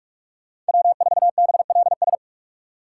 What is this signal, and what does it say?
Morse code, heard as a keyed tone, about 59 wpm, spaced as for 52 wpm, 700 Hz. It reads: W4BLS
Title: Speed Racing - USA 1x3 Call Signs 45wpm